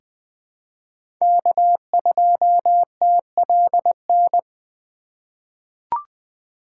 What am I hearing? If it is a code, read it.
Morse code, heard as a keyed tone, about 21 wpm, 700 Hz. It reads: K2TLN
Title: USA Call Signs - 5 Characters - 4x Character Spacing - 20wpm